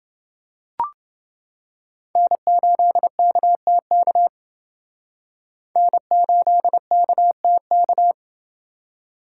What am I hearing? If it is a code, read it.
Morse code, heard as a keyed tone, about 28 wpm, 700 Hz. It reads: N8KTK N8KTK